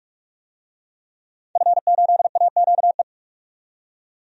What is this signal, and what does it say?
Morse code, heard as a keyed tone, about 42 wpm, 700 Hz. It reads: F8AQE
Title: DXCC-100 Call Signs - 5 Characters - 45wpm